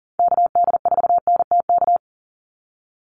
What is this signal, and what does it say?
Morse code, heard as a keyed tone, about 40 wpm, 700 Hz. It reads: KD4NTK
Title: USA Call Signs 40wpm